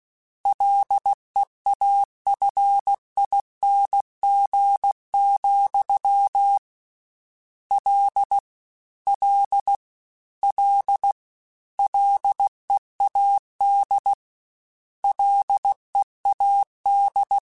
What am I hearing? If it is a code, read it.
Morse code, heard as a keyed tone, about 16 wpm, 800 Hz. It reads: LEAFING, L L L LEAD LEAD
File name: Combinations / Combinations-mp3s / Combinations@15